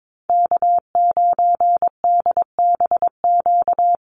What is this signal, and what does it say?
Morse code, heard as a keyed tone, about 22 wpm, 700 Hz. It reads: K9DBQ
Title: USA Call Signs Repeated 3x 22wpm